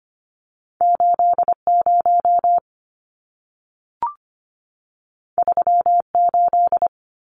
Morse code, heard as a keyed tone, about 25 wpm, 700 Hz. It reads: 80 38
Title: Numbers - 2-Digit 25wpm